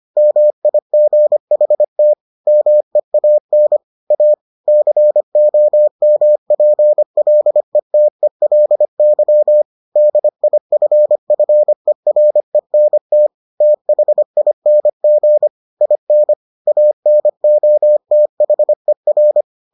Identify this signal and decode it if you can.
Morse code, heard as a keyed tone, about 25 wpm, 600 Hz. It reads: MIGHT MEAN A COMPLETELY DIFFERENT THING IN ANOTHER